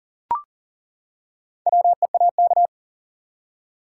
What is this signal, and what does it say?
Morse code, heard as a keyed tone, about 40 wpm, 700 Hz. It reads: WEAK
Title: Four Letter Words - 40wpm